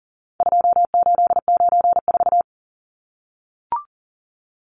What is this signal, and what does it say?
Morse code, heard as a keyed tone, about 37 wpm, 700 Hz. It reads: 2894
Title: Numbers - 4-Digit 40wpm